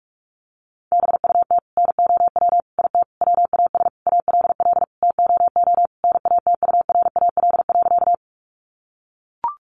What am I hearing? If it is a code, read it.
Morse code, heard as a keyed tone, about 44 wpm, 700 Hz. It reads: BUT NOW IT WAS ALL TOO NATURAL.
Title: Sentences from Top 700 Words 45wpm